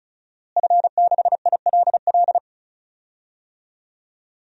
Morse code, heard as a keyed tone, about 36 wpm, 700 Hz. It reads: F6ILL
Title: DXCC-100 Call Signs - 5 Characters - Repeated 2x - 35wpm